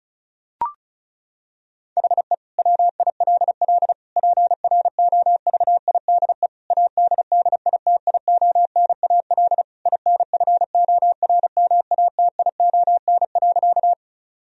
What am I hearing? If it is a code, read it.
Morse code, heard as a keyed tone, about 34 wpm, 700 Hz. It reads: HE WILL PROVIDE ADDITIONAL INFORMATION.
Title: Sentences from Top 1000 Words - Review Part 4 - 35wpm